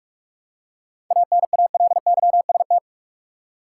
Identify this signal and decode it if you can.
Morse code, heard as a keyed tone, about 46 wpm, 700 Hz. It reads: ANALYST
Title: Practice QSO - Exchange 3 - More Info - 45wpm